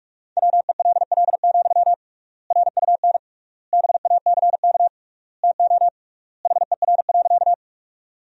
Morse code, heard as a keyed tone, about 46 wpm, 700 Hz. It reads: WELL, RUN BACK TO HER.
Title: 3x Word-Spacing - Sentences from Top 300 Words - 45wpm